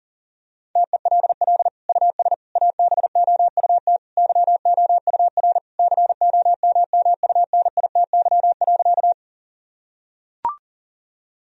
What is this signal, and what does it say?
Morse code, heard as a keyed tone, about 40 wpm, 700 Hz. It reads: TELL US ABOUT YOUR COMMUNITY.
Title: Sentences from Top 1000 Words - Review Part 4 - 40wpm